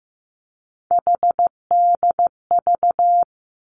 Morse code, heard as a keyed tone, about 15 wpm, 700 Hz. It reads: HDV
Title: Three Challenging Letter-Number Combinations 15wpm